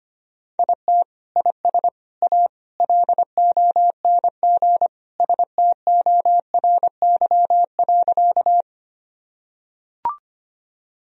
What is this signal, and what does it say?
Morse code, heard as a keyed tone, about 25 wpm, 700 Hz. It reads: IT IS A LONG STORY.
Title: Sentences from Top 1000 Words - Review Part 1 - 25wpm